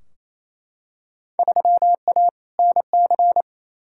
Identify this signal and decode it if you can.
Morse code, heard as a keyed tone, about 28 wpm, 700 Hz. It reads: 3A NC
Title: Field Day Exchanges Encore - 28wpm